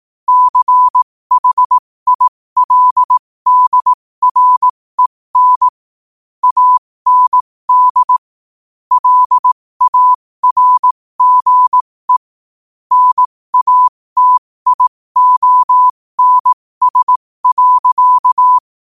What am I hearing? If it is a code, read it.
Morse code, heard as a keyed tone, about 18 wpm, spaced as for 15 wpm, 1000 Hz. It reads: CHILDREN AND LARGE NATIONS.
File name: qotdfiles / 15wpm